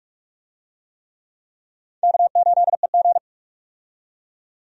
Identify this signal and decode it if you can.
Morse code, heard as a keyed tone, about 46 wpm, 700 Hz. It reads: K8EG
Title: USA Call Signs - 4 Characters - 45wpm